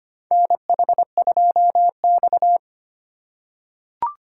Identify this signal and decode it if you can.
Morse code, heard as a keyed tone, about 25 wpm, 700 Hz. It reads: NH2X
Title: USA Call Signs - 4 Characters - 3x Character Spacing - 25wpm